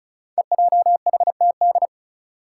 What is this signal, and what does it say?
Morse code, heard as a keyed tone, about 36 wpm, 700 Hz. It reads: EJHTD
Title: Five Challenging Letter-Number Combinations 35wpm